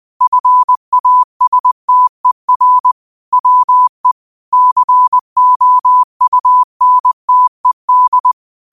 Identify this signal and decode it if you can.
Morse code, heard as a keyed tone, about 20 wpm, 1000 Hz. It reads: FASTER WE COUNTED
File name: qotdfiles / 20wpm